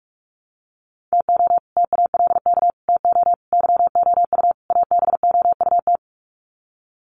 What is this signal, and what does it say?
Morse code, heard as a keyed tone, about 44 wpm, 700 Hz. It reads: TO TALK TO YOU ABOUT
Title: Sets of 5 Words Encore 45wpm